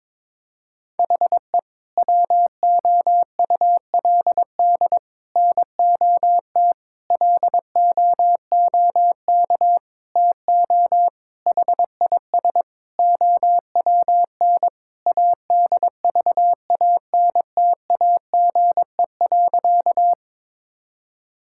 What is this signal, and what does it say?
Morse code, heard as a keyed tone, about 22 wpm, 700 Hz. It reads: HE WOULD NOT LOOK TO HIS OWN ADVANTAGE.
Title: Sentences from Top 1000 Words - Review Part 6 - 22wpm